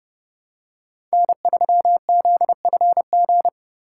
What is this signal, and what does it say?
Morse code, heard as a keyed tone, about 28 wpm, 700 Hz. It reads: N3ZFG